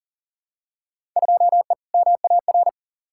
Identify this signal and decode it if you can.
Morse code, heard as a keyed tone, about 37 wpm, 700 Hz. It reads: 2E MAR
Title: Field Day Exchanges Encore - 40wpm